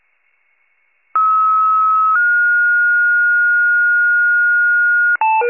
Two-tone page, heard as a one-second tone, then a three-second tone, about 1290 then 1430 Hz. It is heard from 1.1 to 5.2 s.